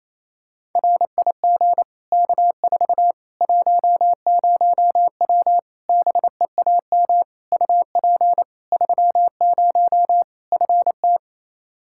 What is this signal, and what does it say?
Morse code, heard as a keyed tone, about 28 wpm, 700 Hz. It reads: RIG K4 10W BEAM UP 30 FT
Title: K1USN SST QSO Practice 28wpm